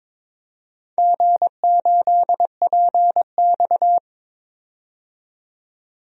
Morse code, heard as a keyed tone, about 22 wpm, 700 Hz. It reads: G8PX
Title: DXCC-100 Call Signs - 4 Characters - 22wpm